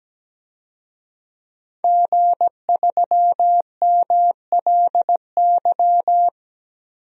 Morse code, heard as a keyed tone, about 17 wpm, 700 Hz. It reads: G3MLY
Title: DXCC-100 Call Signs - 5 Characters - 17wpm